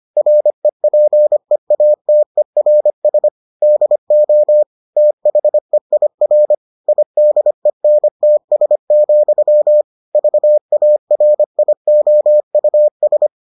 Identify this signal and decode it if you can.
Morse code, heard as a keyed tone, about 25 wpm, 600 Hz. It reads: REPEATERS DO THEIR IDENTS, VARIOUS